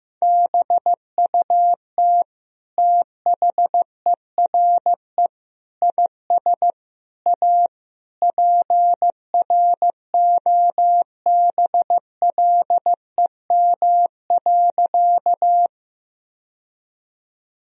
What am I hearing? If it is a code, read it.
Morse code, heard as a keyed tone, about 15 wpm, 700 Hz. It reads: BUT THERE IS A PROBLEM.
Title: Sentences from Top 1000 Words - Review Part 2 - 15wpm